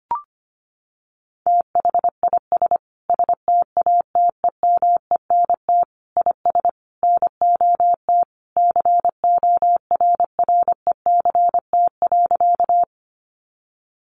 Morse code, heard as a keyed tone, about 25 wpm, 700 Hz. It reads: THIS STATEMENT IS NOT CORRECT.
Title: Sentences from Top 1200 Words 25wpm